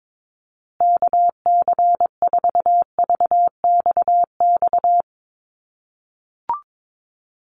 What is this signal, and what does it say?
Morse code, heard as a keyed tone, about 22 wpm, 700 Hz. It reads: KC4VXX
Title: USA Call Signs Repeated 2x 22wpm